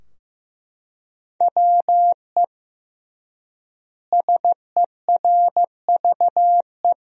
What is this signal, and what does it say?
Morse code, heard as a keyed tone, about 16 wpm, 700 Hz. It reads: WE SERVE
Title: 3x Word-Spacing - Sentences from Top 400 Words - 15wpm